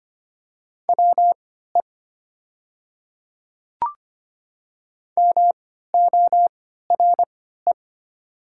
Morse code, heard as a keyed tone, about 25 wpm, spaced as for 14 wpm, 700 Hz. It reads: WE MORE
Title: 3x Character Spacing - Top 100 Words - 25wpm